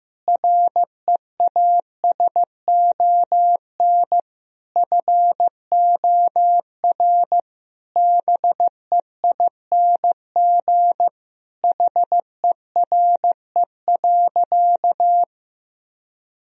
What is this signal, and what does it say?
Morse code, heard as a keyed tone, about 15 wpm, 700 Hz. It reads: REASON FOR BEING HERE.
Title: Sentences from Top 1000 Words - Review Part 2 - 15wpm